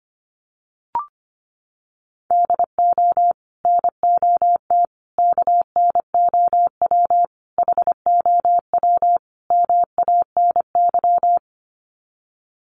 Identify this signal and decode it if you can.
Morse code, heard as a keyed tone, about 25 wpm, 700 Hz. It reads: DO NOT KNOW HOW MANY
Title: Sets of 5 Words Encore 25wpm